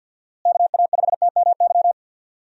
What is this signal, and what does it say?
Morse code, heard as a keyed tone, about 51 wpm, 700 Hz. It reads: KA5TMY
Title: USA Call Signs - 6 Characters - Repeated 3x - 50wpm